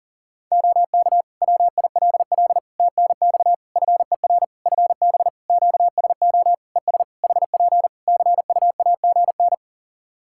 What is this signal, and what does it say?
Morse code, heard as a keyed tone, about 40 wpm, 700 Hz. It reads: OK WILL TNX FER FB QSO ES HP CUAGN
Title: Practice QSO - Exchange 4 - Ending - 40wpm